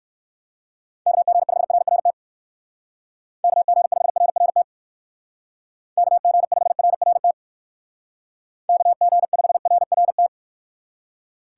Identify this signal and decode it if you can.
Morse code, heard as a keyed tone, about 58 wpm, 700 Hz. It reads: KG5RRT KG5RRT KG5RRT KG5RRT